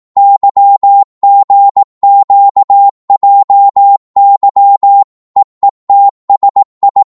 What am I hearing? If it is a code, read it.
Morse code, heard as a keyed tone, about 18 wpm, 800 Hz. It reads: YGQJY EETSI